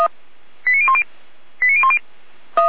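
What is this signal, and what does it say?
Five-tone selective call (ZVEI), heard as quick runs of five tones, tones of about 70 ms. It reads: 89010 89010
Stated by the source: ZVEI five-tone selective call received on an SDR at 455.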